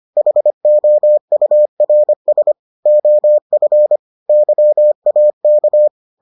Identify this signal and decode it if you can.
Morse code, heard as a keyed tone, about 25 wpm, 600 Hz. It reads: HOURS OF YAK